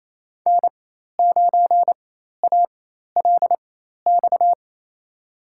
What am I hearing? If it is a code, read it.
Morse code, heard as a keyed tone, about 28 wpm, spaced as for 13 wpm, 700 Hz. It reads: N9ALX
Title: USA Call Signs - 5 Characters - 4x Character Spacing - 28wpm